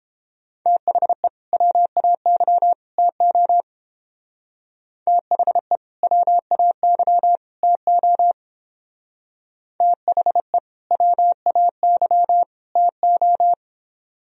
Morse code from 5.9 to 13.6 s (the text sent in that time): WAY TO THE WAY TO